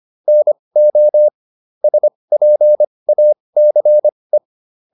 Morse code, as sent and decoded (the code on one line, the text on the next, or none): -. --- / ... .--. .- -.-. .
NO SPACE